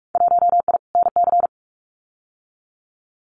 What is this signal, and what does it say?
Morse code, heard as a keyed tone, about 44 wpm, 700 Hz. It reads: 1I NC
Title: Winter Field Day Exchanges 45wpm